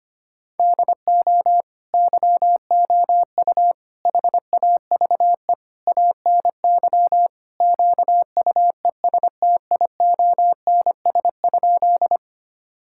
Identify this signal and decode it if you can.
Morse code, heard as a keyed tone, about 25 wpm, 700 Hz. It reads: DO YOU HAVE ANY QUESTIONS?
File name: practice / 25wpm / Sentences with 4-Word Phrases - 25wpm